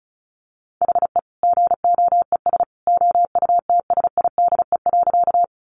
Morse code, heard as a keyed tone, about 34 wpm, 700 Hz. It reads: HE GOES OUTSIDE.
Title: Additional Easy Phrases 35wpm